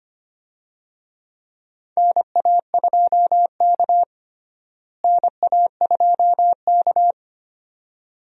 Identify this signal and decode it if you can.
Morse code, heard as a keyed tone, about 25 wpm, 700 Hz. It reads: NA2K NA2K
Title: USA Call Signs - 4 Characters - Repeated 2x - 25wpm